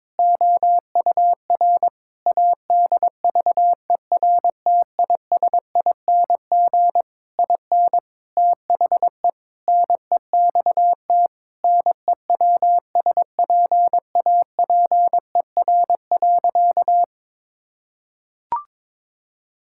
Morse code, heard as a keyed tone, about 22 wpm, 700 Hz. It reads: OUR ADVERTISING IN THE NEXT NEWSPAPER.
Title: Sentences from Top 1000 Words - Review Part 6 - 22wpm